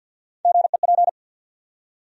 Morse code, heard as a keyed tone, about 51 wpm, 700 Hz. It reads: GEP
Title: USA Call Sign Postfixes - 3 Letters 50wpm